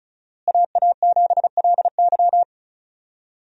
Morse code, heard as a keyed tone, about 36 wpm, 700 Hz. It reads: AA7LY
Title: USA Call Signs - 5 Characters - Repeated 3x - 35wpm